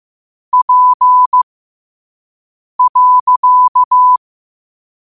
Morse code, heard as a keyed tone, about 15 wpm, spaced as for 5 wpm, 1000 Hz. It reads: P.
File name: qotdfiles / 5wpm